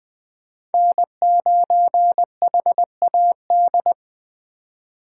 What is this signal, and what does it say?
Morse code, heard as a keyed tone, about 20 wpm, 700 Hz. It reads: N9HAD